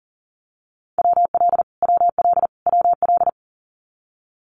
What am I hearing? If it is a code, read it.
Morse code, heard as a keyed tone, about 40 wpm, 700 Hz. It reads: WL WL WL
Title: Abbreviations Repeated 3x - 40wpm